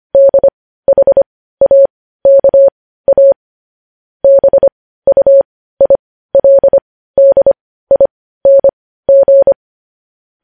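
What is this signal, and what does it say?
Morse code, heard as a keyed tone, about 25 wpm, spaced as for 15 wpm, 550 Hz. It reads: DHAKA BUILDING
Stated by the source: Morse news headlines practice at 15 wpm.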